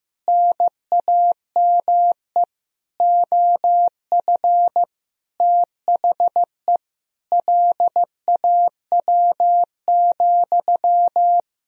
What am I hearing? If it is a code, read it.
Morse code, heard as a keyed tone, about 16 wpm, 700 Hz. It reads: NAME OF THE LAW,
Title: Sentences from Top 1000 Words - Review Part 3 - 15wpm